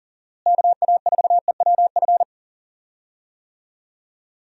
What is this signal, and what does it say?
Morse code, heard as a keyed tone, about 40 wpm, 700 Hz. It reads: KA4EWF
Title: USA Call Signs - 6 Characters - Repeated 3x - 40wpm